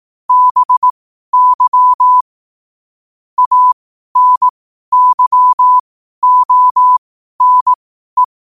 Morse code, heard as a keyed tone, about 18 wpm, spaced as for 12 wpm, 1000 Hz. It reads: BY ANYONE